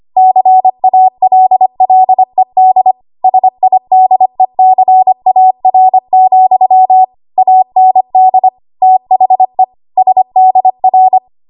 Morse code, heard as a keyed tone, about 25 wpm, 750 Hz. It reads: CALLED SIDECAR, AND THE SDR
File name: Archive / 25 WPM / 25WPM